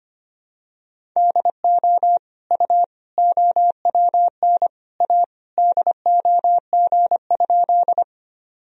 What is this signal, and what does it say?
Morse code, heard as a keyed tone, about 25 wpm, 700 Hz. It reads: DO U OWN A DOG?
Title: Sentences from Top 400 Words and Abbreviations 25wpm